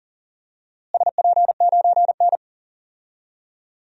Morse code, heard as a keyed tone, about 40 wpm, 700 Hz. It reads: SP9N
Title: DXCC-100 Call Signs - 4 Characters - Repeated 2x - 40wpm